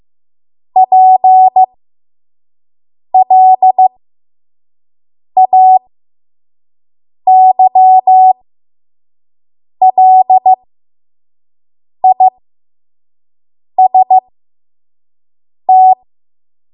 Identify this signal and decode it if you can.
Morse code, heard as a keyed tone, about 15 wpm, spaced as for 5 wpm, 750 Hz. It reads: PLAYLIST
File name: Morse / 05WPM